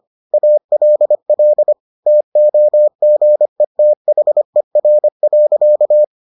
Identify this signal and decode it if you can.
Morse code, heard as a keyed tone, about 25 wpm, 600 Hz. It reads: ALL TOGETHER.